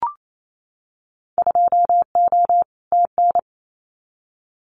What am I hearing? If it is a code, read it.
Morse code, heard as a keyed tone, about 28 wpm, 700 Hz. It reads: 2O TN
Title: Winter Field Day Exchanges 28wpm